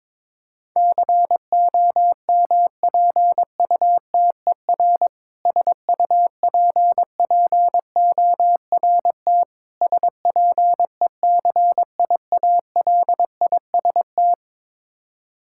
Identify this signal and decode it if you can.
Morse code, heard as a keyed tone, about 22 wpm, 700 Hz. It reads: COMPUTER SUPPORT SPECIALIST
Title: Practice QSO - Exchange 3 - More Info - 22wpm